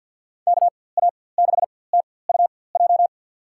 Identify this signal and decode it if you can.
Morse code, heard as a keyed tone, about 47 wpm, spaced as for 23 wpm, 700 Hz. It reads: KA6TUJ
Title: USA Call Signs - 6 Characters - 4x Character Spacing - 50wpm